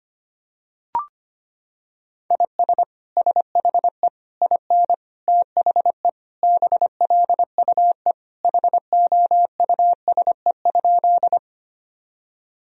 Morse code, heard as a keyed tone, about 25 wpm, 700 Hz. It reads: IS SHE IN THE BLUE HOUSE?
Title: Sentences from Top 1000 Words - Review Part 3 - 25wpm